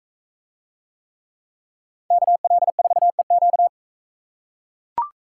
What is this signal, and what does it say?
Morse code, heard as a keyed tone, about 40 wpm, 700 Hz. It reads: KL4EQ